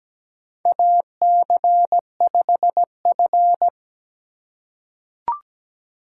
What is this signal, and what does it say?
Morse code, heard as a keyed tone, about 17 wpm, 700 Hz. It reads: AC5F
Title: USA Call Signs - 4 Characters - 4x Character Spacing - 17wpm